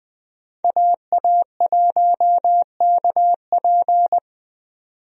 Morse code, heard as a keyed tone, about 20 wpm, 700 Hz. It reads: AA1KP
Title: USA Call Signs - 5 Characters - 20wpm